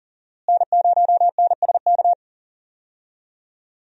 Morse code, heard as a keyed tone, about 40 wpm, 700 Hz. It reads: N0NSK
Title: USA Call Signs - 5 Characters - Repeated 2x - 40wpm